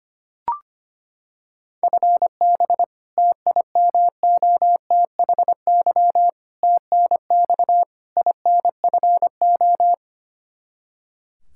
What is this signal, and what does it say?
Morse code, heard as a keyed tone, about 25 wpm, 700 Hz. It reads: FB TIMOTHY TNX INFO
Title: K1USN SST QSO Practice 25wpm